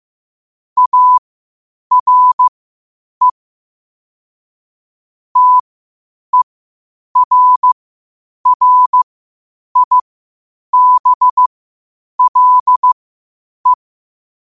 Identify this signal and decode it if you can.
Morse code, heard as a keyed tone, about 15 wpm, spaced as for 8 wpm, 1000 Hz. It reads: ARE TERRIBLE
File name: qotdfiles / 8wpm